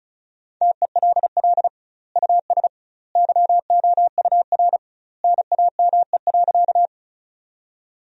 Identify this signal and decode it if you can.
Morse code, heard as a keyed tone, about 36 wpm, 700 Hz. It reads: TELL US YOUR NAME.
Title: 2x Word-Spacing - Sentences from Top 200 Words - 35wpm